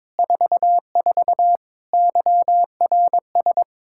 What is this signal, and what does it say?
Morse code, heard as a keyed tone, about 22 wpm, 700 Hz. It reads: 44 YRS
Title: Practice QSO - Exchange 3 - More Info - 22wpm